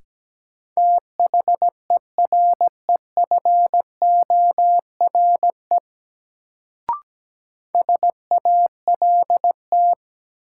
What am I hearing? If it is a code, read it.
Morse code, heard as a keyed tone, about 17 wpm, 700 Hz. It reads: THEREFORE SALT
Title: Sentences from Top 800 Words - New Word Drill - 17wpm